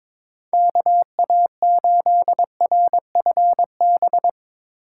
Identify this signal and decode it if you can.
Morse code, heard as a keyed tone, about 24 wpm, 700 Hz. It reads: KA8RFB
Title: Speed Racing - USA 2x3 Call Signs 22wpm